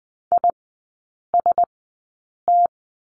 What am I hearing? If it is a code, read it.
Morse code, heard as a keyed tone, about 20 wpm, spaced as for 8 wpm, 700 Hz. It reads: IST
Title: 035 Lesson 7e - Letter s - Three Characters 20wpm